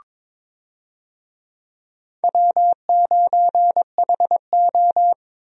Morse code, heard as a keyed tone, about 22 wpm, 700 Hz. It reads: W9HO
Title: USA Call Signs - 4 Characters - 22wpm